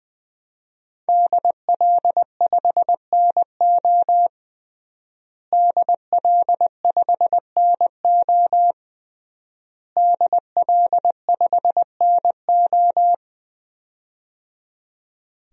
Morse code, heard as a keyed tone, about 20 wpm, 700 Hz. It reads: DL5NO DL5NO DL5NO
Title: DXCC-100 Call Signs - 5 Characters - Repeated 3x - 20wpm